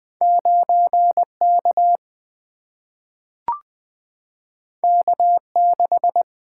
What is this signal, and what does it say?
Morse code, heard as a keyed tone, about 20 wpm, 700 Hz. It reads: 9K K6
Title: Single-Letter Single-Number 20wpm